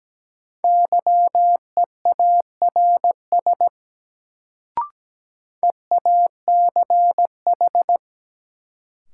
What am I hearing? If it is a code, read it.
Morse code, heard as a keyed tone, about 17 wpm, 700 Hz. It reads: YEARS EACH
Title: Sets of 2 Words - New Word Drill - 17wpm